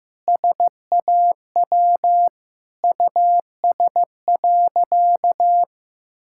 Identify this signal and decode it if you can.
Morse code, heard as a keyed tone, about 16 wpm, 700 Hz. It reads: SAW US.
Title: Sentences from Top 1600 Words 15wpm